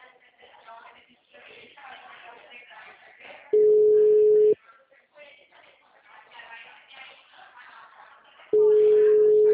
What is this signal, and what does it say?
Ringback tone (Europe, 425 Hz), with speech.